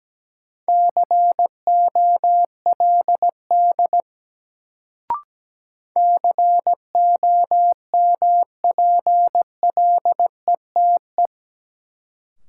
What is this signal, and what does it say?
Morse code, heard as a keyed tone, about 17 wpm, 700 Hz. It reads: COLD COMPLETE
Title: Sentences from Top 400 Words - New Word Drill - 17wpm